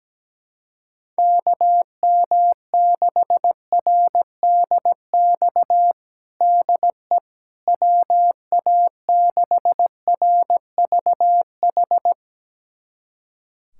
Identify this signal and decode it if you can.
Morse code, heard as a keyed tone, about 17 wpm, 700 Hz. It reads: KM6RDX DE WA6RVH
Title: USA Callsign DE Callsign - 6 Characters - 17wpm